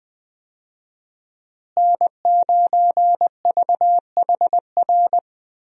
Morse code, heard as a keyed tone, about 20 wpm, 700 Hz. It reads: N9VHR
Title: USA Call Signs - 5 Characters - Repeated 3x - 20wpm